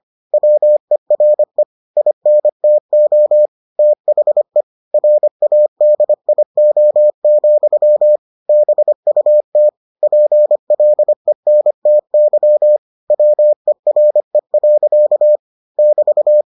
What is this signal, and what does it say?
Morse code, heard as a keyed tone, about 25 wpm, 600 Hz. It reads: WERE INTO THE RADIO, BUT PLENTY WERE. <BT>